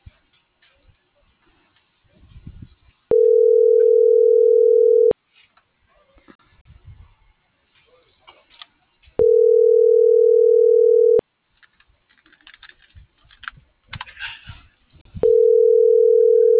Ringback tone (North America), 440 + 480 Hz.